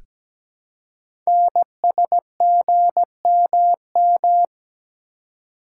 Morse code, heard as a keyed tone, about 17 wpm, 700 Hz. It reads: NSGMM
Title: Code Groups - 5 Letters - 17wpm